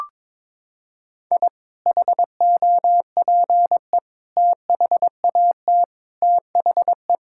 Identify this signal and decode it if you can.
Morse code, heard as a keyed tone, about 22 wpm, 700 Hz. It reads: I HOPE THAT THE
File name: practice / 22wpm / Sentences with 2-Word Phrases - 22wpm